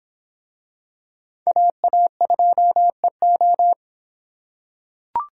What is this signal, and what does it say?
Morse code, heard as a keyed tone, about 26 wpm, 700 Hz. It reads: AA2EO